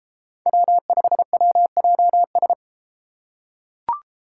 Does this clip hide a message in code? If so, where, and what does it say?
Yes: Morse code from 0.5 to 2.5 s, heard as a keyed tone, about 32 wpm, 700 Hz. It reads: W5WJS